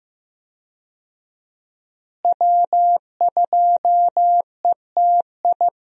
Morse code, heard as a keyed tone, about 15 wpm, 700 Hz. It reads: W2ETI
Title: USA Call Signs - 5 Characters - Repeated 2x - 15wpm